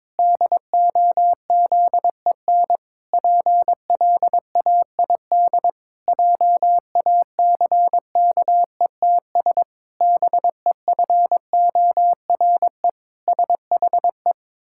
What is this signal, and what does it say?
Morse code, heard as a keyed tone, about 21 wpm, 700 Hz. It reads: DOZEN PLAID JACKETS BEFORE SHE
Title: Pangram Challenge v2 - 22wpm